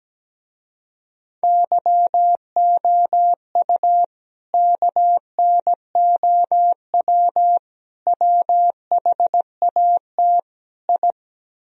Morse code, heard as a keyed tone, about 17 wpm, 700 Hz. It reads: YOU KNOW WHAT I
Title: Speed-Racing - Sets of 4 Words 17wpm